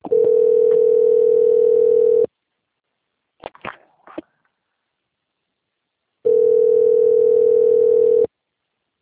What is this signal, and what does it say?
Ringback tone (North America), 440 + 480 Hz.
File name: out- - - - - .5688